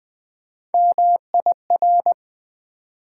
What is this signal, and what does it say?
Morse code, heard as a keyed tone, about 20 wpm, 700 Hz. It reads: MIR
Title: Build A Word - 3 Letters At A Time - 6-Letter Words - 20wpm